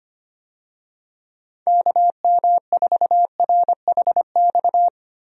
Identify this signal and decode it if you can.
Morse code, heard as a keyed tone, about 25 wpm, 700 Hz. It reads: KM4RHX
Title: USA Call Signs - 6 Characters - Repeated 3x - 25wpm